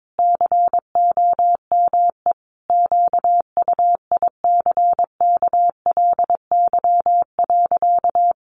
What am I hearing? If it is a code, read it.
Morse code, heard as a keyed tone, about 22 wpm, 700 Hz. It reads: COME QUICKLY.
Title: Sentences from Top 700 Words 22wpm